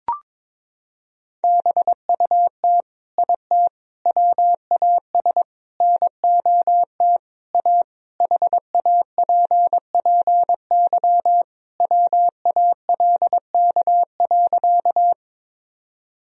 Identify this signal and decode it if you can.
Morse code, heard as a keyed tone, about 22 wpm, 700 Hz. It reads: BUT IT WAS NOT A HAPPY WALK.
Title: Sentences from Top 1000 Words - Review Part 1 - 22wpm